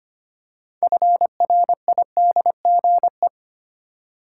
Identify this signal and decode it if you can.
Morse code, heard as a keyed tone, about 25 wpm, 690 Hz. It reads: FRIDGE
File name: practice / 25wpm / Easy Words with Hints - 25wpm